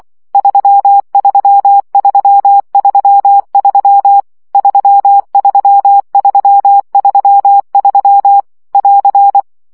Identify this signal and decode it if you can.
Morse code, heard as a keyed tone, about 24 wpm, 800 Hz. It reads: 33333 33333 <AR>
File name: CW gyakorlas 3 20 baud